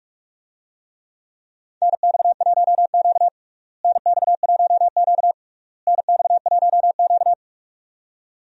Morse code, heard as a keyed tone, about 46 wpm, 700 Hz. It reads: NX1Q NX1Q NX1Q
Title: USA Call Signs - 4 Characters - Repeated 3x - 45wpm